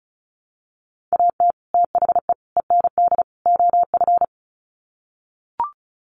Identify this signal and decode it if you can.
Morse code, heard as a keyed tone, about 34 wpm, 700 Hz. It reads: AT THE END OF
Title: Sets of 4 Words 35wpm